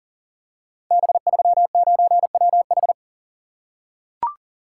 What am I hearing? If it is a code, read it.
Morse code, heard as a keyed tone, about 38 wpm, 700 Hz. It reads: B39WH
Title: Code Groups - 5 Letter:Numbers - 40wpm